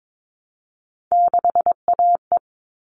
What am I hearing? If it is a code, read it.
Morse code, heard as a keyed tone, about 22 wpm, 700 Hz. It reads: 6AE
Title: All Cut Numbers - 3-Digit 22wpm